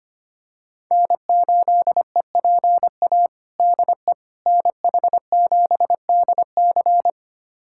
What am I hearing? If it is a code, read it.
Morse code, heard as a keyed tone, about 25 wpm, 700 Hz. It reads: N8EPA DE NH7DC
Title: USA Callsign DE Callsign - 5 Characters - 25wpm